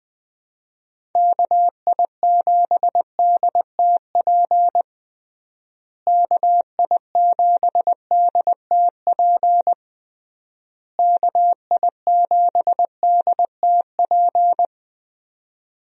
Morse code, heard as a keyed tone, about 20 wpm, 700 Hz. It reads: KI7DTP KI7DTP KI7DTP
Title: USA Call Signs - 6 Characters - Repeated 3x - 20wpm